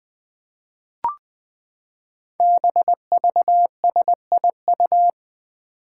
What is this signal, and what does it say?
Morse code, heard as a keyed tone, about 20 wpm, 700 Hz. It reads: BVSIU
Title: Five Challenging Letter-Number Combinations 20wpm